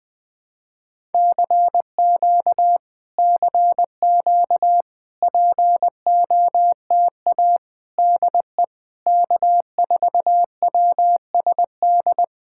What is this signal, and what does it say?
Morse code, heard as a keyed tone, about 21 wpm, 700 Hz. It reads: CQ CQ POTA DE K4WSD
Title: POTA QSO Practice - 20wpm